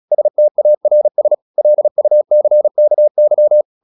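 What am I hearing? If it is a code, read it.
Morse code, heard as a keyed tone, about 36 wpm, 600 Hz. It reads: STARS LUCKY
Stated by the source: Morse practice at 36 wpm, text of Ken Follett's novel Never, from CW med Gnister.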